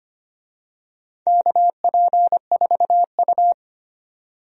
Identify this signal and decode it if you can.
Morse code, heard as a keyed tone, about 25 wpm, 700 Hz. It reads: KP4U
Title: DXCC-100 Call Signs - 4 Characters - 25wpm